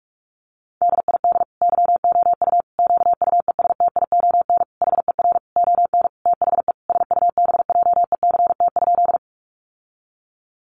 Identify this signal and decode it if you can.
Morse code, heard as a keyed tone, about 44 wpm, 700 Hz. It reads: DID YOU QUESTION HER ON THE SUBJECT?
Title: Sentences from Top 800 Words 45wpm